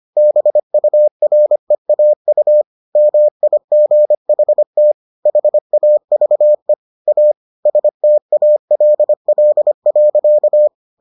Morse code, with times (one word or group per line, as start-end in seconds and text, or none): 0.2-2.6 BUREAU
3.0-4.9 MIGHT
5.3-6.7 HAVE
7.1-7.3 A
7.7-10.7 STALL.